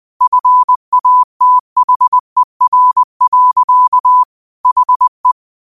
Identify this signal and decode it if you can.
Morse code, heard as a keyed tone, about 20 wpm, 1000 Hz. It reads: FATHER. HE
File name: qotdfiles / 20wpm